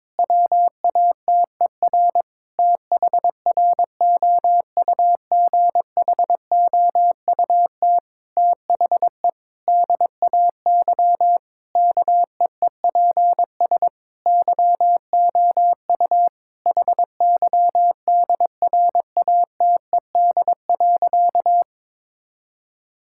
Morse code, heard as a keyed tone, about 22 wpm, 700 Hz. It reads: WATER THROUGHOUT THE DAY KEEPS YOU HYDRATED.
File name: Difficult Sentences with Hints - 22wpm